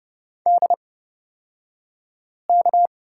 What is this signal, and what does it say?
Morse code, heard as a keyed tone, about 30 wpm, spaced as for 5 wpm, 700 Hz. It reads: DK
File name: Lesson 36d - BT - Two Characters 30wpm